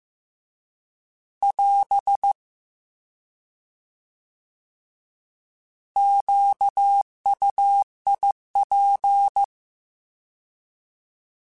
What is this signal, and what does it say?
Morse code, heard as a keyed tone, about 15 wpm, 800 Hz. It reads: <AS> QUIP